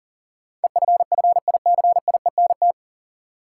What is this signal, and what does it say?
Morse code, heard as a keyed tone, about 40 wpm, 700 Hz. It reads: EFFICIENT
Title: Sentences from Top 1300 Words - New Word Drill - 40wpm